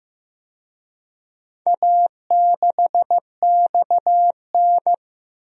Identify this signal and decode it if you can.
Morse code, heard as a keyed tone, about 15 wpm, 700 Hz. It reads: A6XN
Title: DXCC-100 Call Signs - 4 Characters - Repeated 3x - 15wpm